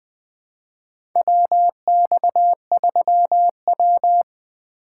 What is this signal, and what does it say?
Morse code, heard as a keyed tone, about 20 wpm, 700 Hz. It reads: WX3W
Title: USA Call Signs - 4 Characters - 4x Character Spacing - 20wpm